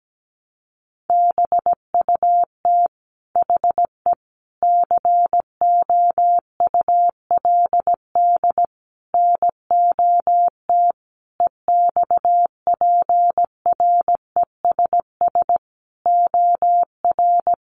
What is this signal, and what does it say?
Morse code, heard as a keyed tone, about 17 wpm, 700 Hz. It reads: BUT HE COULD NOT EXPRESS OR
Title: Sentences from Top 1100 Words 17wpm